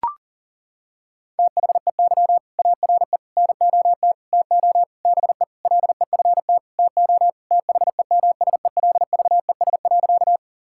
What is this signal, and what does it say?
Morse code, heard as a keyed tone, about 40 wpm, 700 Hz. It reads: THEY ARE NOT TO BE LEFT TO THEMSELVES.
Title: Sentences from Top 1000 Words - Review Part 5 - 40wpm